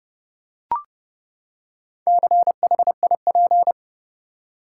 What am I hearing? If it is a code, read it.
Morse code, heard as a keyed tone, about 30 wpm, 700 Hz. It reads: CHIP
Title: Four Letter Words - 30wpm